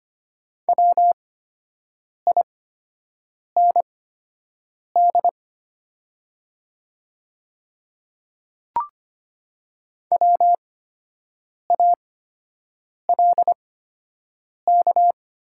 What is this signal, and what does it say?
Morse code, heard as a keyed tone, about 25 wpm, spaced as for 7 wpm, 690 Hz. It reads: WIND WALK
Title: 8x Character Spacing - Top 500 Words - 25wpm